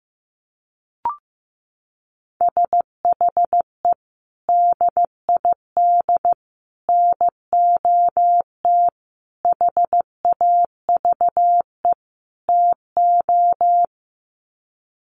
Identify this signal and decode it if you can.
Morse code, heard as a keyed tone, about 15 wpm, 700 Hz. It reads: SHE DID NOT HAVE TO
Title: Sets of 5 Words Encore2 15wpm